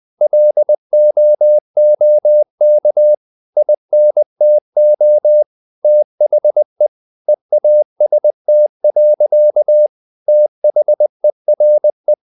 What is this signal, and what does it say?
Morse code, heard as a keyed tone, about 20 wpm, 600 Hz. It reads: LOOK INTO THE EAST. THERE
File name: the old man and the sea / the old man and the sea part 1 20wpm